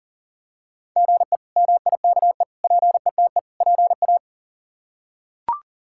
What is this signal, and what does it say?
Morse code, heard as a keyed tone, about 40 wpm, 700 Hz. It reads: GE MIKE PETE PA
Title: K1USN SST QSO Practice - 40wpm